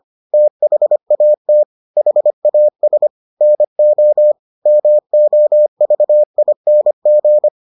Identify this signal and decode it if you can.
Morse code, heard as a keyed tone, about 25 wpm, 600 Hz. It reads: THAT HAS NO MOVING